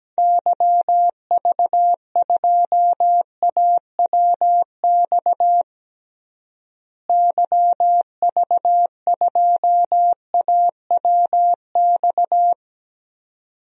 Morse code, heard as a keyed tone, about 17 wpm, 700 Hz. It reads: YV2AWX YV2AWX
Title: DXCC-100 Call Signs - 6 Characters - Repeated 2x - 17wpm